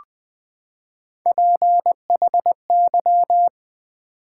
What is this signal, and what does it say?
Morse code, heard as a keyed tone, about 20 wpm, 700 Hz. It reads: PHY
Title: Build A Word - 3 Letters At A Time - 9-Letter Words - 20wpm